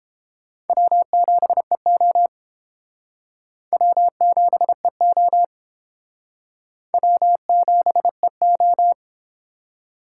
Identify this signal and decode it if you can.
Morse code, heard as a keyed tone, about 30 wpm, 700 Hz. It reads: W7EO W7EO W7EO